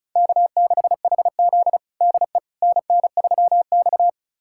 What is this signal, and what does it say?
Morse code, heard as a keyed tone, about 34 wpm, 700 Hz. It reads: K6HZ DE NN3X
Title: USA Callsign DE Callsign - 4 Characters - 35wpm